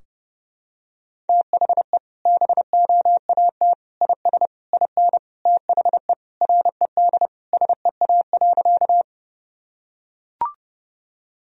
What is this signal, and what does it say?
Morse code, heard as a keyed tone, about 32 wpm, 700 Hz. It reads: THE BOAT IS IN THE RED SEA.
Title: Sentences from Top 1000 Words - Review Part 2 - 30wpm